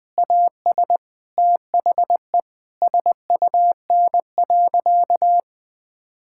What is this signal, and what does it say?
Morse code, heard as a keyed tone, about 20 wpm, 700 Hz. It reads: AS THE SUN.
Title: Difficult Sentences with Hints - 20wpm